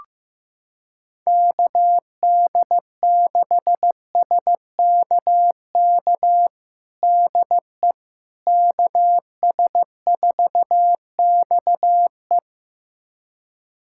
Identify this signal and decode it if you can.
Morse code, heard as a keyed tone, about 16 wpm, 700 Hz. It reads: KD6SKK DE KS4XE
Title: USA Callsign DE Callsign 15wpm